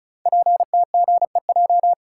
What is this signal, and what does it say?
Morse code, heard as a keyed tone, about 36 wpm, 700 Hz. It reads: PTGEJ
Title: Code Groups - 5 Letters - 35wpm